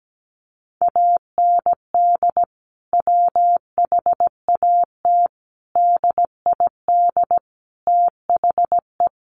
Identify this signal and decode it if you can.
Morse code, heard as a keyed tone, about 17 wpm, 700 Hz. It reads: AND WHAT DID THE
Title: Sentences from Top 1500 Words 17wpm